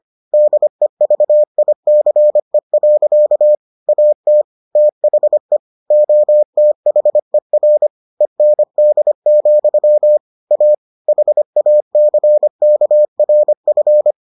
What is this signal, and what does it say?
Morse code, heard as a keyed tone, about 25 wpm, 600 Hz. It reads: DEVICE. AT THE OTHER END, A HACKRF